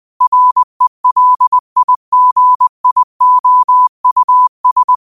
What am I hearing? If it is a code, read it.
Morse code, heard as a keyed tone, about 20 wpm, 1000 Hz. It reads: RELIGIOUS
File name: qotdfiles / 20wpm